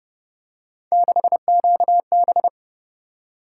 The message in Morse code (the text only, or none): 6QB